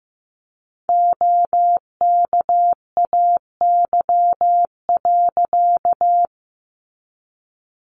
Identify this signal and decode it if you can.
Morse code, heard as a keyed tone, about 15 wpm, 700 Hz. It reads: OKAY.
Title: A Sound Of Thunder 15wpm